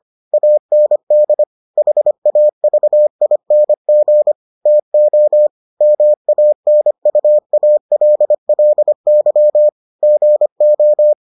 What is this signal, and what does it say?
Morse code, heard as a keyed tone, about 25 wpm, 600 Hz. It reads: AND HAVING TO MANUALLY GO